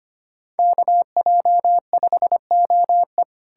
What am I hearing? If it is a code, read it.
Morse code, heard as a keyed tone, about 33 wpm, 700 Hz. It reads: KJ5OE